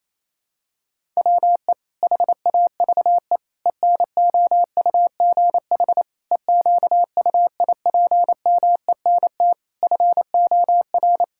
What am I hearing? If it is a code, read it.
Morse code, heard as a keyed tone, about 27 wpm, 700 Hz. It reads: WE HAVE ENOUGH EQUIPMENT FOR
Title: Sentences from Top 1000 Words - Review Part 5 - 28wpm